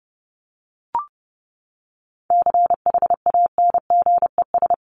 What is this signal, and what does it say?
Morse code, heard as a keyed tone, about 29 wpm, 700 Hz. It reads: CHANGES
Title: ES:S Suffix - Top 100 Words with Base Words 30wpm